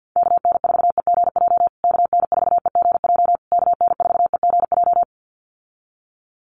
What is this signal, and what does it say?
Morse code, heard as a keyed tone, about 50 wpm, 700 Hz. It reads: KN4EGJ KN4EGJ KN4EGJ
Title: USA Call Signs Repeated 3x 50wpm